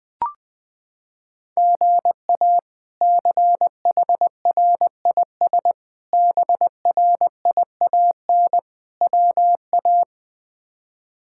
Morse code, heard as a keyed tone, about 20 wpm, 700 Hz. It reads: GA CHRIS BRIAN WA
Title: K1USN SST QSO Practice - 20wpm